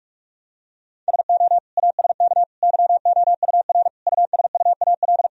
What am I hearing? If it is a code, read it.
Morse code, heard as a keyed tone, about 46 wpm, 700 Hz. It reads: SO ASK YOUR USUAL
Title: Sentences from Top 1000 Words - Review Part 2 - 45wpm